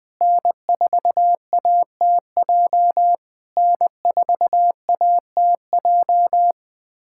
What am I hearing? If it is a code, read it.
Morse code, heard as a keyed tone, about 20 wpm, 700 Hz. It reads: N4ATJ N4ATJ
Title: POTA QSO Practice - 20wpm